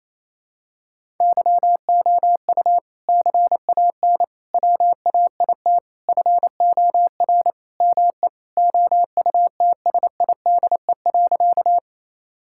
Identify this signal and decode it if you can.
Morse code, heard as a keyed tone, about 27 wpm, 700 Hz. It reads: YOU CAN WAIT FOR ME OUTSIDE.
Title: Sentences from Top 1000 Words - Review Part 6 - 28wpm